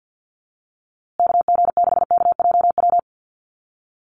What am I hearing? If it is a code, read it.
Morse code, heard as a keyed tone, about 50 wpm, 700 Hz. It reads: KG6KJW
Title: USA Call Signs Repeated 2x 50wpm